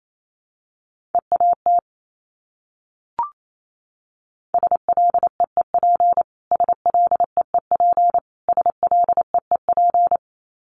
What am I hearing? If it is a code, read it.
Morse code, heard as a keyed tone, about 28 wpm, 700 Hz. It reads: EAT SLEEP SLEEP SLEEP
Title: Top 500 Words Repeated 28wpm